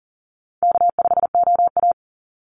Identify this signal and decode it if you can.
Morse code, heard as a keyed tone, about 40 wpm, 700 Hz. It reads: K5OA
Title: Easy Call Signs Repeated 3x - 40wpm